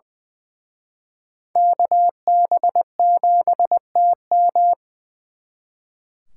Morse code, heard as a keyed tone, about 24 wpm, 700 Hz. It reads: KB7TM
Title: Speed Racing - USA Call Signs 20wpm